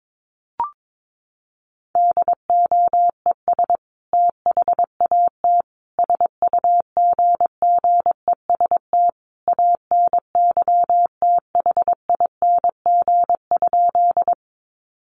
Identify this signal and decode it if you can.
Morse code, heard as a keyed tone, about 22 wpm, 700 Hz. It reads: DOES THAT SUGGEST ANYTHING?
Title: Sentences from Top 1300 Words 22wpm